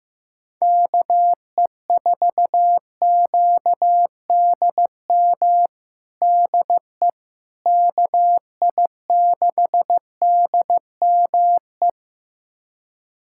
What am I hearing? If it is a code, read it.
Morse code, heard as a keyed tone, about 15 wpm, 700 Hz. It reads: KE4QDM DE KI6DME
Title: USA Callsign DE Callsign - 6 Characters - 15wpm